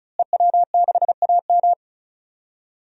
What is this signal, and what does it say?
Morse code, heard as a keyed tone, about 36 wpm, 700 Hz. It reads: EW6AM
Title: DXCC-100 Call Signs - 5 Characters - Repeated 3x - 35wpm